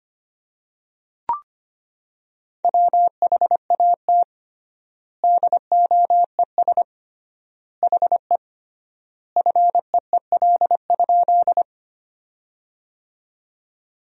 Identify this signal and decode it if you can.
Morse code, heard as a keyed tone, about 25 wpm, 700 Hz. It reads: WHAT DOES HE FEEL?
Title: 3x Word-Spacing - Sentences from Top 400 Words - 25wpm